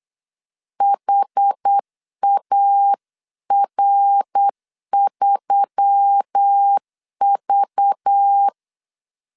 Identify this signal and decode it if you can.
Morse code, heard as a keyed tone, about 8 wpm, 800 Hz. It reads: HAR3V
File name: CW / DET / 1 9 1500C S C 8wpm 5653